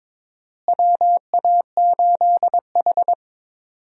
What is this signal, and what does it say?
Morse code, heard as a keyed tone, about 22 wpm, 700 Hz. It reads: WA8H
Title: USA Call Signs - 4 Characters - Repeated 3x - 22wpm